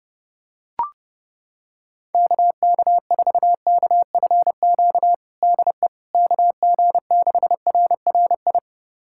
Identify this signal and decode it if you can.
Morse code, heard as a keyed tone, about 30 wpm, 700 Hz. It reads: KK4KFQ DE KG6RRI
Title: USA Callsign DE Callsign - 6 Characters - 30wpm